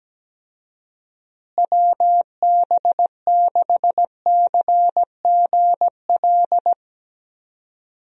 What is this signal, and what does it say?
Morse code, heard as a keyed tone, about 17 wpm, 700 Hz. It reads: WB6CGL
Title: USA Call Signs - 6 Characters - Repeated 2x - 17wpm